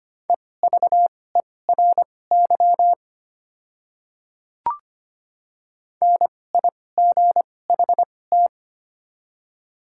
Morse code, heard as a keyed tone, about 25 wpm, spaced as for 18 wpm, 700 Hz. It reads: EVERY NIGHT
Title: 2x Character Spacing - Top 300 Words - 25wpm